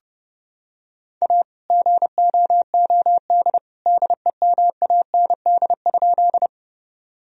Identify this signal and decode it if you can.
Morse code, heard as a keyed tone, about 30 wpm, 700 Hz. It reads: A GOOD DEMAND?
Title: Sentences from Top 1000 Words - Review Part 4 - 30wpm